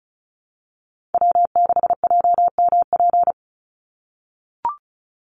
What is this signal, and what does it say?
Morse code, heard as a keyed tone, about 34 wpm, 700 Hz. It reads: W6JMP
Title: NA SOTA Chaser Callsigns 35wpm